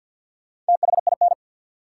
Morse code, heard as a keyed tone, about 52 wpm, 700 Hz. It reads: THIN
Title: Sentences from Top 1200 Words - New Word Drill - 50wpm